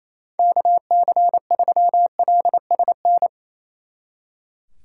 Morse code, heard as a keyed tone, about 34 wpm, 700 Hz. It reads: KC3LSN
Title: Speed Racing - USA 2x3 Call Signs 28wpm